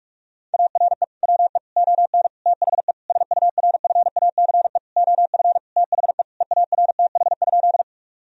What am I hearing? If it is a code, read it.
Morse code, heard as a keyed tone, about 41 wpm, 700 Hz. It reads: ARE WE ON THE SURFACE OF THE EARTH?
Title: Sentences from Top 1000 Words - Review Part 3 - 45wpm